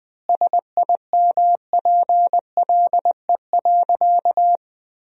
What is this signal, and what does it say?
Morse code, heard as a keyed tone, about 20 wpm, 700 Hz. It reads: SIMPLE.
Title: 4x Word-Spacing - Sentences from Top 500 Words - 20wpm